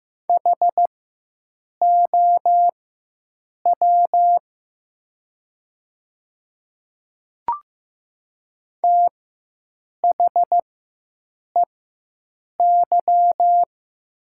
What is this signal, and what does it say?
Morse code, heard as a keyed tone, about 15 wpm, spaced as for 7 wpm, 700 Hz. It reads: HOW THEY
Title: 4x Character Spacing - Top 100 Words - 15wpm